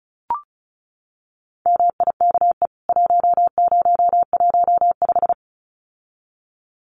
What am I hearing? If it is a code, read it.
Morse code, heard as a keyed tone, about 34 wpm, 700 Hz. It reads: MIKE 1015
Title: CWT Exchange 35wpm